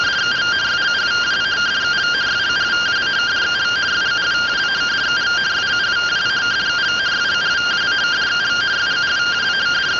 Two-tone teleprinter signal, about 170 Hz shift.